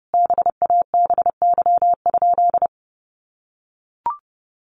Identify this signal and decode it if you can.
Morse code, heard as a keyed tone, about 30 wpm, 700 Hz. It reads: BABY?
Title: Sentences from Top 1300 Words 30wpm